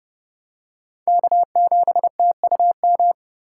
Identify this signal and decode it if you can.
Morse code, heard as a keyed tone, about 32 wpm, 700 Hz. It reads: K7TUM
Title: USA Call Signs - 5 Characters - 3x Character Spacing - 30wpm